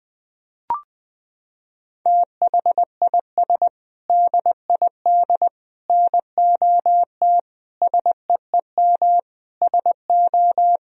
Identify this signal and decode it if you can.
Morse code, heard as a keyed tone, about 20 wpm, 700 Hz. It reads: THIS DID NOT SEEM SO
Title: Sentences from Top 1000 Words - Review Part 1 - 20wpm